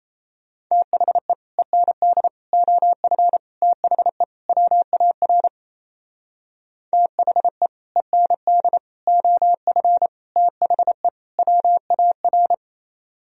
Morse code, heard as a keyed tone, about 34 wpm, 700 Hz. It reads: THE END OF THE WAR THE END OF THE WAR
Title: Speed-Racing - Sets of 5 Words 28wpm